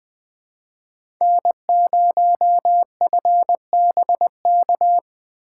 Morse code, heard as a keyed tone, about 20 wpm, 700 Hz. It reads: N0FBK
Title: USA Call Signs - 5 Characters - 3x Character Spacing - 20wpm